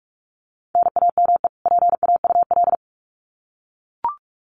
Morse code, heard as a keyed tone, about 44 wpm, 700 Hz. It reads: NAME PAUL
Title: Common QSO Phrases 45wpm